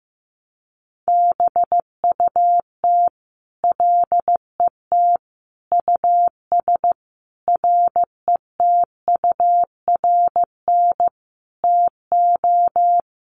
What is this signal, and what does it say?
Morse code, heard as a keyed tone, about 15 wpm, 700 Hz. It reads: BUT LET US RETURN TO
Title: Sentences from Top 800 Words 15wpm